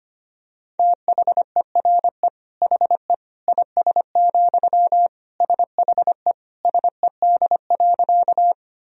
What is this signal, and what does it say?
Morse code, heard as a keyed tone, about 25 wpm, 700 Hz. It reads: THERE HE IS, SHE SED.
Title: Sentences from Top 100 Words and Abbreviations 25wpm